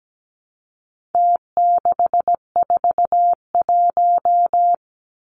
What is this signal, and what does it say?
Morse code, heard as a keyed tone, about 17 wpm, 700 Hz. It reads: T641
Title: Numbers - Contest Style - 4-Digit 17wpm